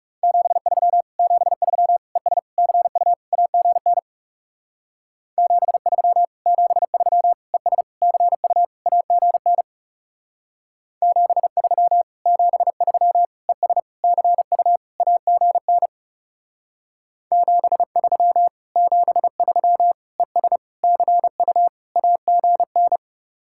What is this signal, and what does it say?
Morse code, heard as a keyed tone, about 39 wpm, 700 Hz. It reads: 73 73 ES CU AGN 73 73 ES CU AGN 73 73 ES CU AGN 73 73 ES CU AGN